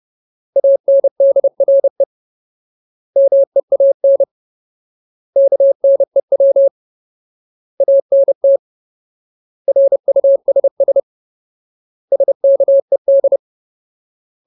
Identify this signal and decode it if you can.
Morse code, heard as a keyed tone, about 30 wpm, 550 Hz. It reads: ANDRE MEAN KNEW ANT RUSS SKED
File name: Ultimate Word List / 30WPM Ultimate